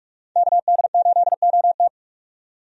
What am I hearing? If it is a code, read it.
Morse code, heard as a keyed tone, about 46 wpm, 700 Hz. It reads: KD8OT
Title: USA Call Signs - 5 Characters - 45wpm